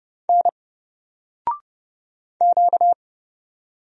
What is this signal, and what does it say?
Morse code, heard as a keyed tone, about 30 wpm, 700 Hz. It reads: N Q
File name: Single-Letter-Number-30wpm